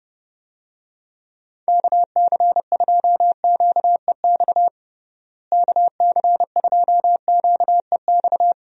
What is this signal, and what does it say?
Morse code, heard as a keyed tone, about 32 wpm, 700 Hz. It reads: KC2QEX KC2QEX
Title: USA Call Signs - 6 Characters - Repeated 2x - 30wpm